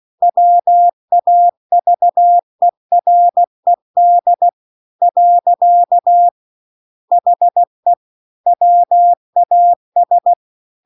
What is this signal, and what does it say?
Morse code, heard as a keyed tone, about 16 wpm, 700 Hz. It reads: WAVERED . HE WAS